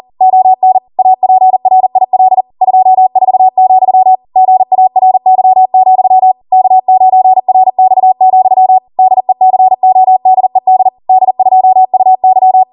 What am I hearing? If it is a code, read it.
Morse code, heard as a keyed tone, about 40 wpm, 750 Hz. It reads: ON APRIL 24, GARY, K9RX, DECODED D2UY